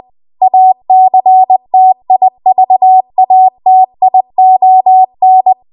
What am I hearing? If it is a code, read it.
Morse code, heard as a keyed tone, about 20 wpm, 750 Hz. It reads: ACTIVATION
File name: Archive / 20 WPM / 20WPM